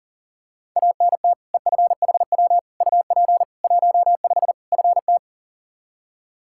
Morse code, heard as a keyed tone, about 40 wpm, 700 Hz. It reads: ANT EFHW UP 15 FT
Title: Practice QSO - Exchange 2 - Info - 40wpm